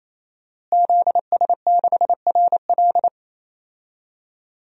Morse code, heard as a keyed tone, about 28 wpm, 700 Hz. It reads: ZS6RL
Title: DXCC-100 Call Signs - 5 Characters - Repeated 3x - 28wpm